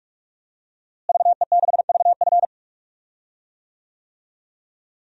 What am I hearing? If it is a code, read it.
Morse code, heard as a keyed tone, about 43 wpm, 700 Hz. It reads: VE6VF
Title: DXCC-100 Call Signs - 5 Characters - Repeated 2x - 45wpm